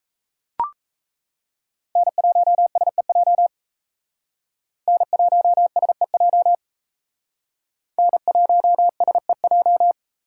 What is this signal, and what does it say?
Morse code, heard as a keyed tone, about 35 wpm, 700 Hz. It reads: N1SEJ N1SEJ N1SEJ